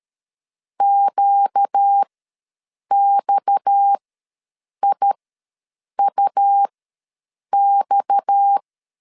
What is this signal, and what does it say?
Morse code, heard as a keyed tone, about 13 wpm, spaced as for 7 wpm, 800 Hz. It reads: QXIUX
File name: LEZIONI / 6 PKX0 / 6 PKX ws 12wpm 3106